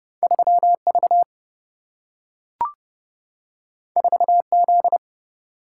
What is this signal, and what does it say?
Morse code, heard as a keyed tone, about 30 wpm, 700 Hz. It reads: 3V 4Z
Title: Single-Letter Single-Number 30wpm